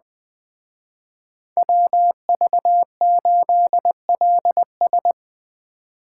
Morse code, heard as a keyed tone, about 20 wpm, 700 Hz. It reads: WV8LS
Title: Speed Racing - USA Call Signs 17wpm